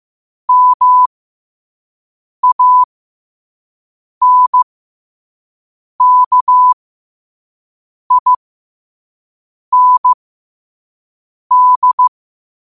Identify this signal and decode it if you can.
Morse code, heard as a keyed tone, about 15 wpm, spaced as for 5 wpm, 1000 Hz. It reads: MANKIND
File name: 5wpm